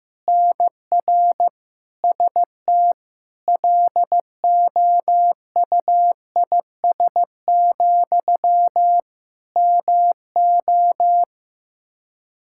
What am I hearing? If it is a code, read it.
Morse code, heard as a keyed tone, about 15 wpm, 700 Hz. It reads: NR ST LOUIS, MO
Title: Practice QSO - Exchange 1 - Beginning - 15wpm